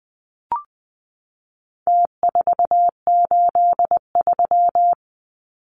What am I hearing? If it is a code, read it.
Morse code, heard as a keyed tone, about 19 wpm, 700 Hz. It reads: T483
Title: Numbers - Contest Style - 4-Digit 20wpm